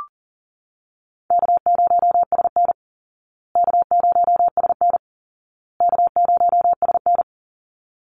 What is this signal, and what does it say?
Morse code, heard as a keyed tone, about 40 wpm, 700 Hz. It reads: K0SN K0SN K0SN
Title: Easy Call Signs Repeated 3x - 40wpm